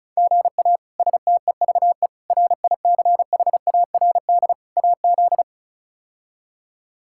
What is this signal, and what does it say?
Morse code, heard as a keyed tone, about 34 wpm, 700 Hz. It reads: GA STEVE RICHARD AZ
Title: K1USN SST QSO Practice - 35wpm